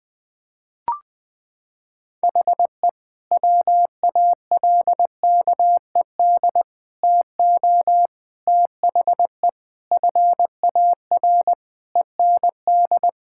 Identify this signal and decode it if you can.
Morse code, heard as a keyed tone, about 20 wpm, 700 Hz. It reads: HE WALKED TO THE FAR END
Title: Sentences with 5-Word Phrases - 20wpm